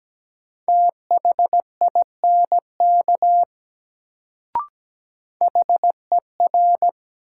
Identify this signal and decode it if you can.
Morse code, heard as a keyed tone, about 17 wpm, 700 Hz. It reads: THINK HER
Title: Sets of 2 Words - New Word Drill - 17wpm